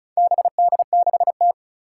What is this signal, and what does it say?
Morse code, heard as a keyed tone, about 36 wpm, 700 Hz. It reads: BD6T
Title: Four Challenging Letter-Number Combinations 35wpm